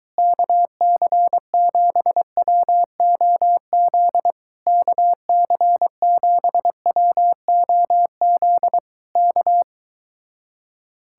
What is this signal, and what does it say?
Morse code, heard as a keyed tone, about 23 wpm, 700 Hz. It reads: KC7WOZ KC7WOZ K